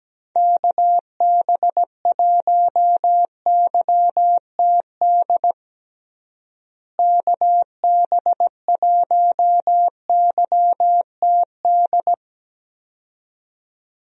Morse code, heard as a keyed tone, about 17 wpm, 700 Hz. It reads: KB1YTD KB1YTD
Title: USA Call Signs - 6 Characters - Repeated 2x - 17wpm